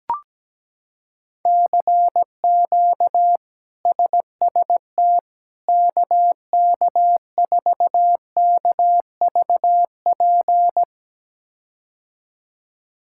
Morse code, heard as a keyed tone, about 17 wpm, 700 Hz. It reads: CQ SST KK4KVP
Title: K1USN SST QSO Practice - 17wpm